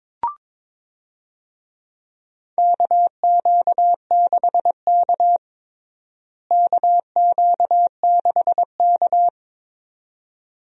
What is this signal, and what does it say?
Morse code, heard as a keyed tone, about 21 wpm, 700 Hz. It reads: KQ6K KQ6K
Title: USA Call Signs - 4 Characters - Repeated 2x - 22wpm